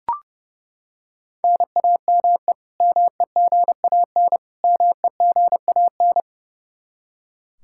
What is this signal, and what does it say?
Morse code, heard as a keyed tone, about 30 wpm, 700 Hz. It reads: NAME MEGAN MEGAN
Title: Practice QSO - Exchange 1 - Beginning - 30wpm